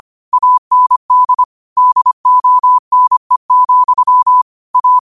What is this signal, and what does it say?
Morse code, heard as a keyed tone, about 25 wpm, 1000 Hz. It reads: AND DONE, A
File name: qotdfiles / 25wpm